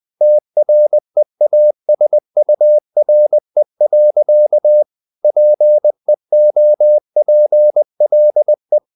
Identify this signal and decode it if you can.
Morse code, heard as a keyed tone, about 20 wpm, 600 Hz. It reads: TREASURE. PEOPLE
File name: CW / Practice / EnglishSayings20